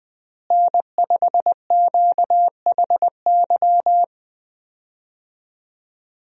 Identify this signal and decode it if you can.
Morse code, heard as a keyed tone, about 21 wpm, 700 Hz. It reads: N5QHY
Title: USA Call Signs - 5 Characters - Repeated 3x - 20wpm